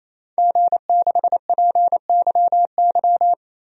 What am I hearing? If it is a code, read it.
Morse code, heard as a keyed tone, about 28 wpm, 700 Hz. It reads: G6PYY
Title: DXCC-100 Call Signs - 5 Characters - Repeated 3x - 28wpm